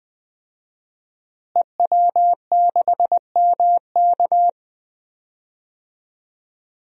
Morse code, heard as a keyed tone, about 20 wpm, 700 Hz. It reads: EW6MK
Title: DXCC-100 Call Signs - 5 Characters - Repeated 3x - 20wpm